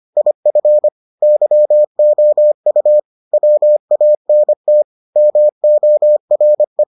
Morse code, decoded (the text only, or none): IF YOU WANT MORE